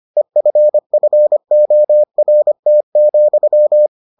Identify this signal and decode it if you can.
Morse code, heard as a keyed tone, about 25 wpm, 600 Hz. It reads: EFFORT,